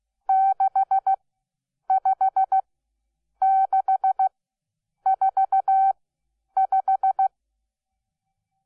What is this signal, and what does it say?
Morse code, heard as a keyed tone, about 15 wpm, spaced as for 8 wpm, 770 Hz. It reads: 65645
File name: Onlinelehrgang / morsen / DJ4UF Morsen12